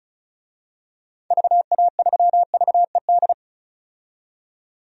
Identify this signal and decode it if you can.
Morse code, heard as a keyed tone, about 36 wpm, 700 Hz. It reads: VA3VED
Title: DXCC-100 Call Signs - 6 Characters - 35wpm